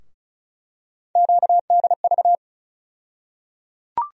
Morse code, heard as a keyed tone, about 34 wpm, 700 Hz. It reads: QDV
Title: USA Call Sign Postfixes - 3 Letters 35wpm